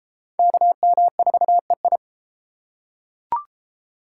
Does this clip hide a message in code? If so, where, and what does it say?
Yes: Morse code from 0.4 to 2.0 s, heard as a keyed tone, about 32 wpm, 700 Hz. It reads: KM4EI